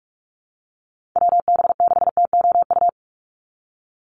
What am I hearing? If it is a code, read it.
Morse code, heard as a keyed tone, about 44 wpm, 700 Hz. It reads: WB6TOU
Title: Call Signs Repeated 2x - 45wpm